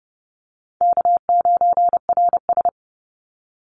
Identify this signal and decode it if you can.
Morse code, heard as a keyed tone, about 30 wpm, 700 Hz. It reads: K9RS
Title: Call Signs - 30wpm